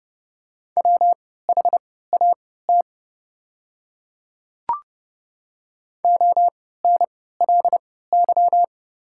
Morse code, heard as a keyed tone, about 30 wpm, spaced as for 17 wpm, 700 Hz. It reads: WHAT ONLY
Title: 3x Character Spacing - Top 200 Words - 30wpm